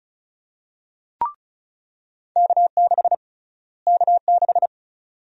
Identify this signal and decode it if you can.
Morse code, heard as a keyed tone, about 36 wpm, 700 Hz. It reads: K6 K6
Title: USA Call Sign Prefixes - Repeated 2x - 35wpm